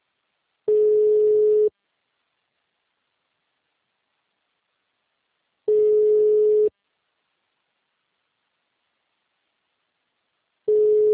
Ringback tone (Europe, 425 Hz).